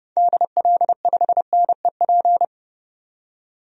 Morse code, heard as a keyed tone, about 31 wpm, 690 Hz. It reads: DL5NEP
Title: DXCC-100 Call Signs - 6 Characters - 30wpm